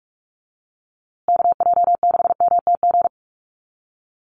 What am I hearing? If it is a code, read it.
Morse code, heard as a keyed tone, about 44 wpm, 700 Hz. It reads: KJ6MTG
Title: USA Call Signs Repeated 3x 45wpm